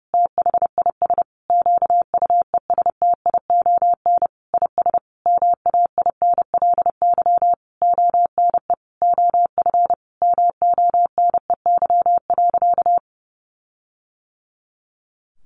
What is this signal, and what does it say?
Morse code, heard as a keyed tone, about 30 wpm, 700 Hz. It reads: THIS QUESTION IS MAINLY ONE OF MONEY.
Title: Sentences from Top 1400 Words 30wpm